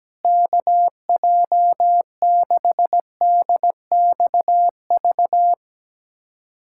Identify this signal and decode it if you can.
Morse code, heard as a keyed tone, about 17 wpm, 700 Hz. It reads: KJ6DXV
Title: USA Callsign DE Callsign 17wpm